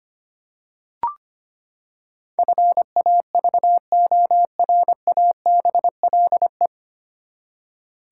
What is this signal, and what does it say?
Morse code, heard as a keyed tone, about 25 wpm, 700 Hz. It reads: FAVORABLE
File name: Nine Letter Words - 25wpm